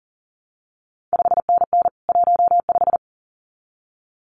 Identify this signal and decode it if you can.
Morse code, heard as a keyed tone, about 39 wpm, 700 Hz. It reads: 5NN 15
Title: CQ WW DX Contest Exchange Encore - 40wpm